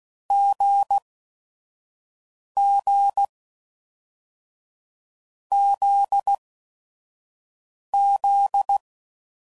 Morse code, heard as a keyed tone, about 16 wpm, spaced as for 4 wpm, 800 Hz. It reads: GGZZ